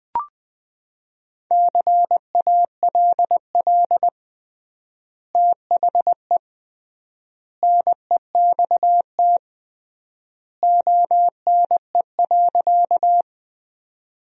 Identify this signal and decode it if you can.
Morse code, heard as a keyed tone, about 20 wpm, 700 Hz. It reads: CALL THE NEXT ONE.
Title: 3x Word-Spacing - Sentences from Top 300 Words - 20wpm